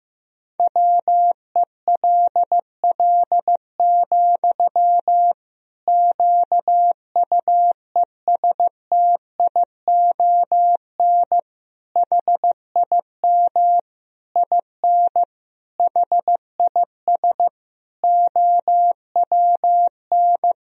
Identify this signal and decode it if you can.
Morse code, heard as a keyed tone, about 15 wpm, 700 Hz. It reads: WELL, QUESTION HIM IN HIS OWN
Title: Sentences from Top 1000 Words - Review Part 3 - 15wpm